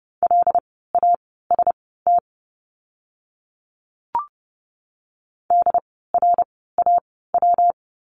Morse code, heard as a keyed tone, about 30 wpm, spaced as for 17 wpm, 700 Hz. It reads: LAST DRAW
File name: course / 30:10wpm / 115 Lesson 19f - Letter w - Words 30wpm